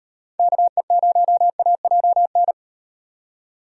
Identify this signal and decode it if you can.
Morse code, heard as a keyed tone, about 38 wpm, 700 Hz. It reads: KE0AJN